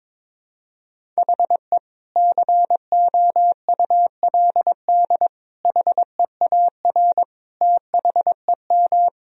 Morse code, heard as a keyed tone, about 22 wpm, 700 Hz. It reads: HE COULD HEAR THEM
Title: Sentences from Top 1000 Words - Review Part 1 - 22wpm